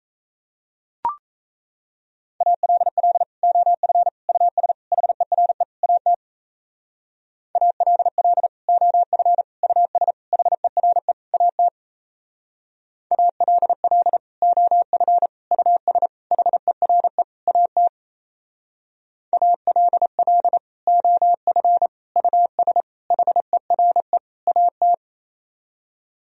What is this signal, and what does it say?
Morse code, heard as a keyed tone, about 34 wpm, 700 Hz. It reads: ALL OF US HERE AT ALL OF US HERE AT ALL OF US HERE AT ALL OF US HERE AT
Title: Speed-Racing - Sets of 5 Words 28wpm